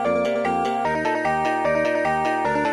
There is no code.